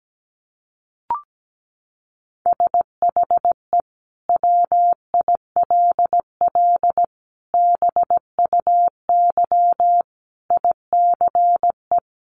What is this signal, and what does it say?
Morse code, heard as a keyed tone, about 17 wpm, 700 Hz. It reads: SHE WILL BUY ICE
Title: Sentences from Top 1500 Words 17wpm